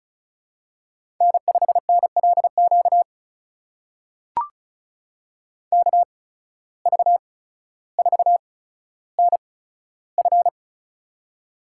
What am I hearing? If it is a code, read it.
Morse code, heard as a keyed tone, about 36 wpm, spaced as for 15 wpm, 700 Hz. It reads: N5NLQ KV4NF
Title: USA Call Signs - 5 Characters - 8x Character Spacing - 35wpm